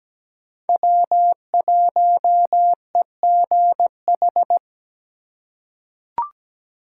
Morse code, heard as a keyed tone, about 17 wpm, 700 Hz. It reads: W1EGH
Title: USA Call Signs - 5 Characters - 2x Character Spacing - 17wpm